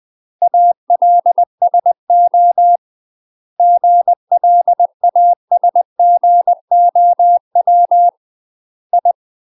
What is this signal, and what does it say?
Morse code, heard as a keyed tone, about 20 wpm, 700 Hz. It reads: ALSO GLASGOW I
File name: UK MASTER QSOs TRAINING / 20WPM / UK MASTER QSOs 0006